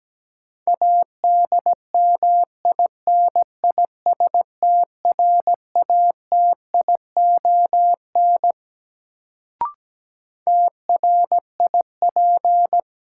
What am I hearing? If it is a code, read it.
Morse code, heard as a keyed tone, about 17 wpm, 700 Hz. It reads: ADMINISTRATION TRIP
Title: Sentences from Top 1300 Words - New Word Drill Alt - 17wpm